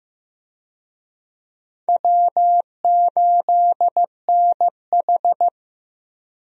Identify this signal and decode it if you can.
Morse code, heard as a keyed tone, about 15 wpm, 700 Hz. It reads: W8NH
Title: USA Call Signs - 4 Characters - 15wpm